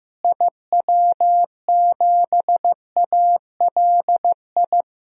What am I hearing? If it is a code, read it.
Morse code, heard as a keyed tone, about 15 wpm, 700 Hz. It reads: IW7ALI
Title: DXCC-100 Call Signs - 6 Characters - Repeated 3x - 15wpm